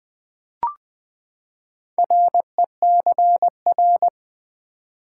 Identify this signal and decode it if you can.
Morse code, heard as a keyed tone, about 20 wpm, 700 Hz. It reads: RECR
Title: Build A Word - 4 Letters At A Time - 11-Letter Words - 20wpm